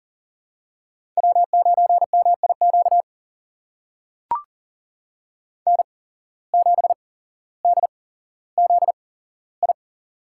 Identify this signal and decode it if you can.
Morse code, heard as a keyed tone, about 38 wpm, spaced as for 17 wpm, 700 Hz. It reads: W9MIQ N7DZI
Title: USA Call Signs - 5 Characters - 8x Character Spacing - 40wpm